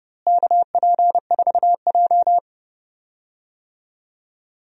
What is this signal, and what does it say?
Morse code, heard as a keyed tone, about 30 wpm, 700 Hz. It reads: KP4J
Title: DXCC-100 Call Signs - 4 Characters - Repeated 2x - 30wpm